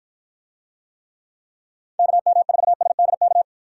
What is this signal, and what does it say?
Morse code, heard as a keyed tone, about 53 wpm, 700 Hz. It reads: KM4SDK